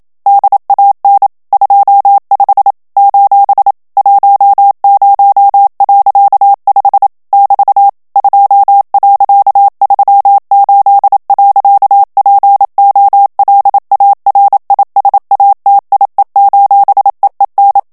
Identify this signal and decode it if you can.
Morse code, heard as a keyed tone, about 27 wpm, 800 Hz. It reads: DAN 25 : 10.5 <BT> 2.38. POLARISATIE: EEN
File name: HOLLANDAIS texte clair 1 vitesse 26 mots minute